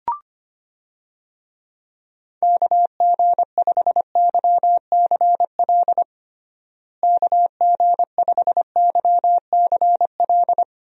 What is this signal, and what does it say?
Morse code, heard as a keyed tone, about 25 wpm, 700 Hz. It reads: KG5YCL KG5YCL
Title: USA Call Signs - 6 Characters - Repeated 3x - 25wpm